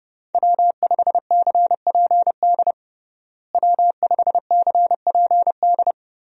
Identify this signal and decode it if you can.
Morse code, heard as a keyed tone, about 30 wpm, 700 Hz. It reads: W5CPD W5CPD
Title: USA Call Signs - 5 Characters - Repeated 3x - 30wpm